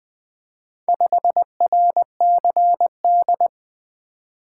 Morse code, heard as a keyed tone, about 21 wpm, 700 Hz. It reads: 5RCD
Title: Four Letter-Number Combinations - 20wpm